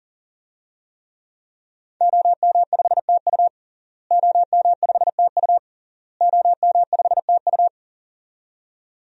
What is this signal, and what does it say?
Morse code, heard as a keyed tone, about 40 wpm, 700 Hz. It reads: OM5TU OM5TU OM5TU
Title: DXCC-100 Call Signs - 5 Characters - Repeated 3x - 40wpm